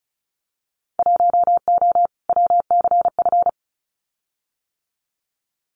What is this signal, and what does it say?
Morse code, heard as a keyed tone, about 34 wpm, 690 Hz. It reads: 1O WCF
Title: Winter Field Day Exchanges 35wpm